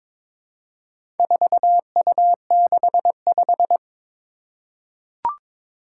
Morse code, heard as a keyed tone, about 22 wpm, 700 Hz. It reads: 4U65
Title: Four Challenging Letter-Number Combinations 22wpm